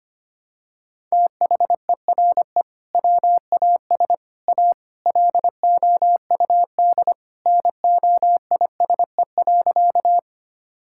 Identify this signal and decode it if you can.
Morse code, heard as a keyed tone, about 25 wpm, 700 Hz. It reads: THERE WAS A LOUD NOISE.
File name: practice / 25wpm / Sentences with 3-Word Phrases - 25wpm